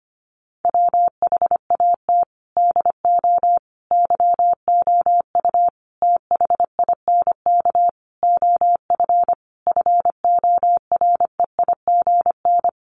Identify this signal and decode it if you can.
Morse code, heard as a keyed tone, about 25 wpm, 700 Hz. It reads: WHAT DO YOU THINK OF FOREIGN
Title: Sentences from Top 1400 Words 25wpm